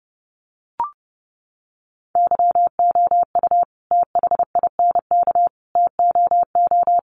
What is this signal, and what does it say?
Morse code, heard as a keyed tone, about 28 wpm, 700 Hz. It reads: YOU THINK TOO
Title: Sentences from Top 1100 Words 30wpm